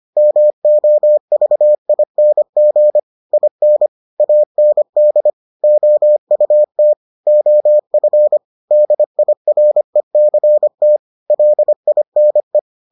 Morse code, heard as a keyed tone, about 25 wpm, 600 Hz. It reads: MOVING IN AND OUT OF DIRECT LINE